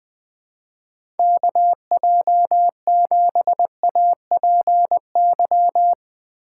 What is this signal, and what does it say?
Morse code, heard as a keyed tone, about 20 wpm, 700 Hz. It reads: KJ7APY
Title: USA Call Signs - 6 Characters - 4x Character Spacing - 20wpm